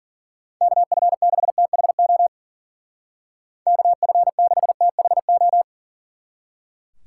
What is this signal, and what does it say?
Morse code, heard as a keyed tone, about 49 wpm, 700 Hz. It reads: KF6THO KF6THO
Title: Speed Racing - USA Call Signs 40wpm